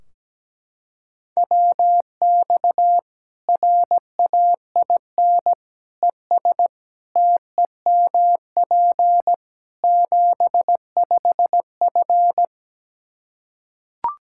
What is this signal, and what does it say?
Morse code, heard as a keyed tone, about 17 wpm, 700 Hz. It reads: WX RAIN ES TEMP 75F
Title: Practice QSO - Exchange 2 - Info - 17wpm